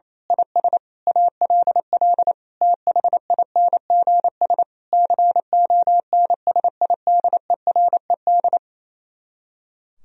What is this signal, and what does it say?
Morse code, heard as a keyed tone, about 26 wpm, 700 Hz. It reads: IS ALL THINGS CONSIDERED
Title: Speed-Racing - Sets of 5 Words 28wpm